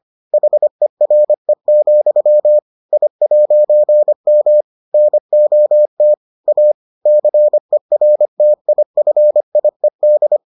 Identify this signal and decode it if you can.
Morse code, heard as a keyed tone, about 25 wpm, 600 Hz. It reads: HERE, I'M NOT A CERTIFIED